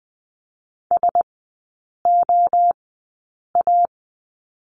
Morse code, heard as a keyed tone, about 20 wpm, spaced as for 8 wpm, 700 Hz. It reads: SOA
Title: 041 Lesson 8e - Number 1 - Three Characters 20wpm